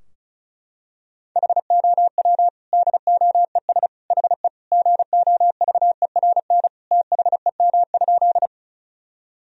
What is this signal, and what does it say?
Morse code, heard as a keyed tone, about 36 wpm, 700 Hz. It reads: HOW DOES HE GOVERN THEM?
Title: Sentences from Top 1000 Words - Review Part 2 - 35wpm